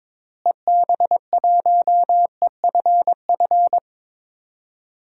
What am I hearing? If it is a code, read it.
Morse code, heard as a keyed tone, about 22 wpm, 700 Hz. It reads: EB1EFF
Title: DXCC-100 Call Signs - 6 Characters - Repeated 3x - 22wpm